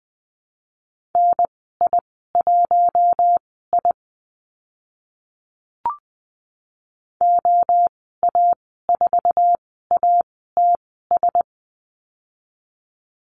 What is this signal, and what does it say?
Morse code, heard as a keyed tone, about 20 wpm, spaced as for 15 wpm, 700 Hz. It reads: NI1I OA4ATS
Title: 055 Lesson 10g - Letter r - Call Signs 20wpm